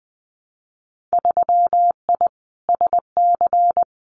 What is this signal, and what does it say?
Morse code, heard as a keyed tone, about 20 wpm, 700 Hz. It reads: 3I SC
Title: Winter Field Day Exchanges Encore 20wpm